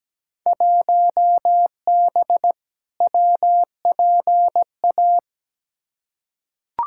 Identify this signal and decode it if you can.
Morse code, heard as a keyed tone, about 17 wpm, 700 Hz. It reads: 1B WPA
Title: Field Day Exchanges Encore - 17wpm